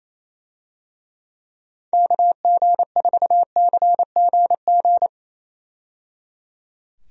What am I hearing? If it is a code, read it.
Morse code, heard as a keyed tone, about 27 wpm, 700 Hz. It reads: KG4CGG
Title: USA Call Signs - 6 Characters - 28wpm